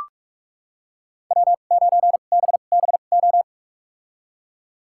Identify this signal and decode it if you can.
Morse code, heard as a keyed tone, about 45 wpm, spaced as for 32 wpm, 700 Hz. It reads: W9BBO
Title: USA Call Signs - 5 Characters - 2x Character Spacing - 45wpm